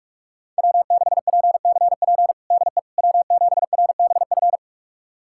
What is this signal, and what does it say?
Morse code, heard as a keyed tone, about 46 wpm, 700 Hz. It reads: W6PCP DE W7RBF
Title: USA Callsign DE Callsign - 5 Characters - 45wpm